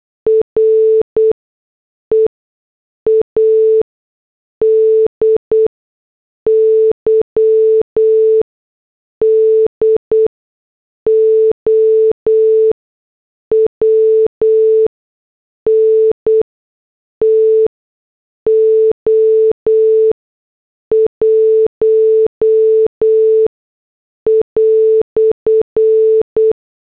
Morse code, heard as a keyed tone, about 8 wpm, spaced as for 6 wpm, 440 Hz. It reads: READY DOWN TO 1"